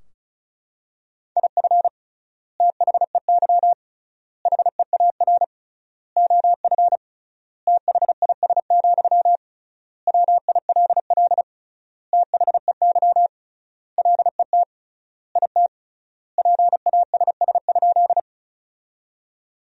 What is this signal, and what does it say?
Morse code, heard as a keyed tone, about 36 wpm, 700 Hz. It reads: IF THEY HEAR OF THIS, WILL THEY LET IT PASS?
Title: 3x Word-Spacing - Sentences from Top 400 Words - 35wpm